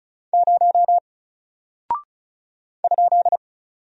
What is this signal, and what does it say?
Morse code, heard as a keyed tone, about 36 wpm, 700 Hz. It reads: 0 ?
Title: Single Character (Letter:Number:Punctuation:Prosign) 35wpm